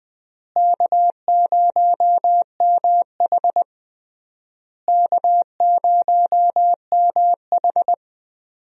Morse code, heard as a keyed tone, about 21 wpm, 700 Hz. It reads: K0MH K0MH
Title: USA Call Signs - 4 Characters - Repeated 3x - 20wpm